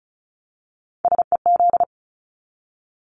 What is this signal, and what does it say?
Morse code, heard as a keyed tone, about 34 wpm, 700 Hz. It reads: SEZ
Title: Abbreviations 35wpm